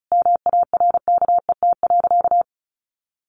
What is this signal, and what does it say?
Morse code, heard as a keyed tone, about 34 wpm, 700 Hz. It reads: MARKET.
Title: Sentences from Top 1400 Words 35wpm